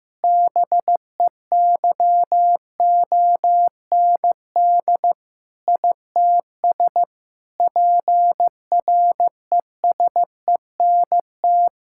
Morse code, heard as a keyed tone, about 15 wpm, 700 Hz. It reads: BEYOND ITS PRESENT
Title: Sentences from Top 1000 Words - Review Part 6 - 15wpm